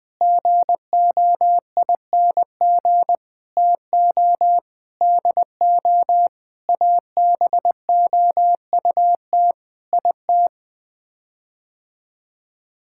Morse code, heard as a keyed tone, about 21 wpm, 700 Hz. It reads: GOING TO DO ABOUT IT
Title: Sets of 8 Words - 20wpm